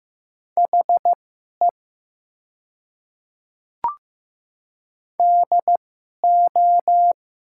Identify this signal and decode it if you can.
Morse code, heard as a keyed tone, about 15 wpm, spaced as for 11 wpm, 700 Hz. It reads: HE DO
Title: 2x Character Spacing - Top 300 Words - 15wpm